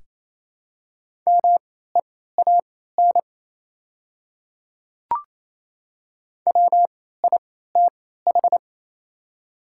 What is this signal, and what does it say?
Morse code, heard as a keyed tone, about 28 wpm, spaced as for 16 wpm, 700 Hz. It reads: MEAN WITH
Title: 3x Character Spacing - Top 300 Words - 28wpm